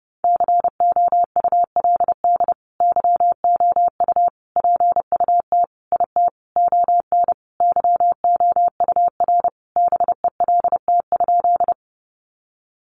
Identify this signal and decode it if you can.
Morse code, heard as a keyed tone, about 30 wpm, 700 Hz. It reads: COULD YOU PUT IT ON YOUR BELT?
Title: Sentences from Top 1400 Words 30wpm